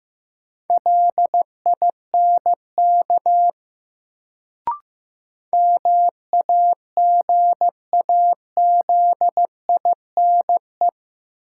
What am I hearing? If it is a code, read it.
Morse code, heard as a keyed tone, about 15 wpm, 700 Hz. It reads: LINK MAGAZINE
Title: Sentences from Top 1000 Words - Review Part 7 - New Word Drill - 15wpm